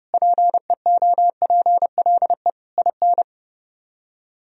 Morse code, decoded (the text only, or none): PEOPLE IN